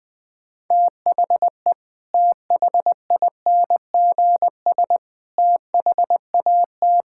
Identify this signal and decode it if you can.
Morse code, heard as a keyed tone, about 21 wpm, 700 Hz. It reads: THE THINGS THAT
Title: Sentences with 3-Word Phrases - Spoken Beginning - 20wpm